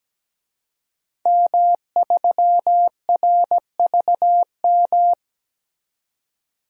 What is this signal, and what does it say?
Morse code, heard as a keyed tone, about 17 wpm, 700 Hz. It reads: M3RVM
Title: DXCC-100 Call Signs - 5 Characters - 17wpm